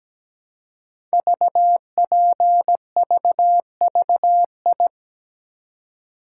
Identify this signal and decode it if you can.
Morse code, heard as a keyed tone, about 17 wpm, 700 Hz. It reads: VPVVI
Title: Code Groups - 5 Letters - 17wpm